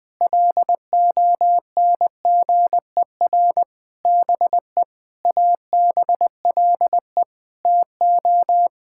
Morse code, heard as a keyed tone, about 21 wpm, 700 Hz. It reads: LONGER BE ABLE TO
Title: Sets of 6 Words - 20wpm